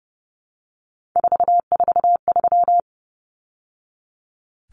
Morse code, heard as a keyed tone, about 30 wpm, 700 Hz. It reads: 443
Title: Numbers - 3-Digit 30wpm